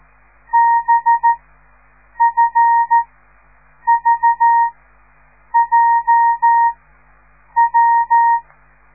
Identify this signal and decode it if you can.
Morse code, heard as a keyed tone, about 13 wpm, spaced as for 7 wpm, 930 Hz. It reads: BFVJW